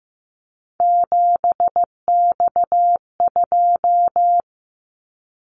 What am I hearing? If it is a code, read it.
Morse code, heard as a keyed tone, about 15 wpm, 700 Hz. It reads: 7X2
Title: Callsign Prefixes for Top 100 DXCC - 15wpm